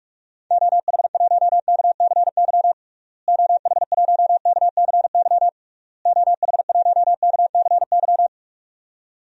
Morse code, heard as a keyed tone, about 46 wpm, 700 Hz. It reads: OH1KCY OH1KCY OH1KCY
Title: DXCC-100 Call Signs - 6 Characters - Repeated 3x - 45wpm